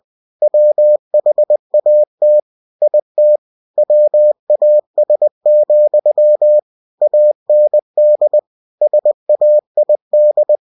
Morse code, heard as a keyed tone, about 20 wpm, 600 Hz. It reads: WHAT IT WAS, AND SAID
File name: cw-books / winnie-the-pooh / winnie-20wpm-page-0074